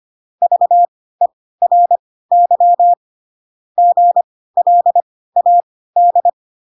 Morse code, heard as a keyed tone, about 25 wpm, spaced as for 16 wpm, 700 Hz. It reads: VERY GLAD
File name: Pinocchio / Pinocchio ch2 10 25 ik1hge